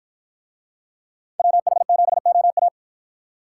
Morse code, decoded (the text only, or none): WH7OA